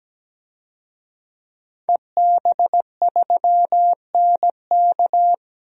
Morse code, heard as a keyed tone, about 17 wpm, 700 Hz. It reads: EB3NK
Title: DXCC-100 Call Signs - 5 Characters - 17wpm